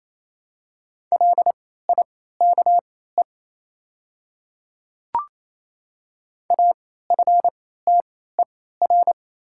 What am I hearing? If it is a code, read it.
Morse code, heard as a keyed tone, about 28 wpm, spaced as for 16 wpm, 700 Hz. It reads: LIKE AFTER
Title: 3x Character Spacing - Top 300 Words - 28wpm